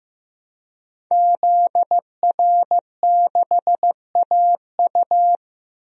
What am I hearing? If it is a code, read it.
Morse code, heard as a keyed tone, about 15 wpm, 700 Hz. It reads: ZR6AU
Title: DXCC-100 Call Signs - 5 Characters - Repeated 3x - 15wpm